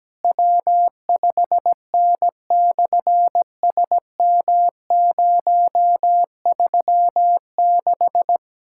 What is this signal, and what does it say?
Morse code, heard as a keyed tone, about 18 wpm, 700 Hz. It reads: W5N/SM036
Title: SOTA QSO Practice 17wpm